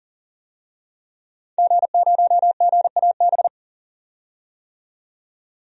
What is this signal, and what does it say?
Morse code, heard as a keyed tone, about 39 wpm, 700 Hz. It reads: G0GAB
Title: DXCC-100 Call Signs - 5 Characters - 40wpm